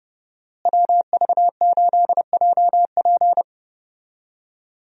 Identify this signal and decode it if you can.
Morse code, heard as a keyed tone, about 32 wpm, 700 Hz. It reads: WV8JP
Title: USA Call Signs - 5 Characters - Repeated 3x - 30wpm